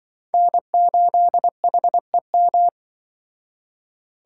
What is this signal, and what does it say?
Morse code, heard as a keyed tone, about 25 wpm, 700 Hz. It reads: N8HEM
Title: Speed Racing - USA Call Signs 20wpm